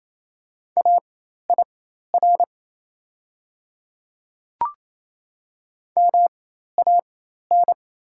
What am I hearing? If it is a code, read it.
Morse code, heard as a keyed tone, about 28 wpm, spaced as for 13 wpm, 690 Hz. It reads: AIR MAN
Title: 4x Character Spacing - Top 200 Words - 28wpm